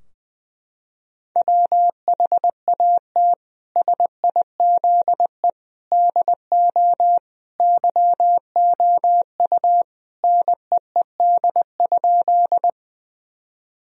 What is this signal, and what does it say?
Morse code, heard as a keyed tone, about 20 wpm, 700 Hz. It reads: WHAT SIZE DO YOU NEED?
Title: Sentences from Top 1000 Words - Review Part 2 - 20wpm